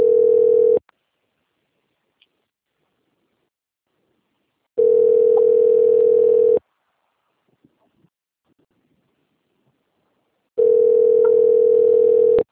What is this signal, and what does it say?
Ringback tone (North America), 440 + 480 Hz.